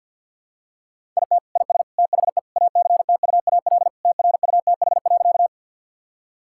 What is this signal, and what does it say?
Morse code, heard as a keyed tone, about 50 wpm, 700 Hz. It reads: IT IS THE ACTUAL TRUTH.
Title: Sentences from Top 1000 Words - Review Part 5 - 50wpm